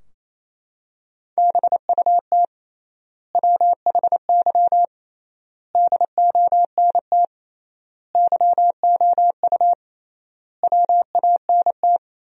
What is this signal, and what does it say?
Morse code, heard as a keyed tone, about 28 wpm, 700 Hz. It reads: BUT WHY DONT YOU WANT
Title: 3x Word-Spacing - Sentences from Top 300 Words - 28wpm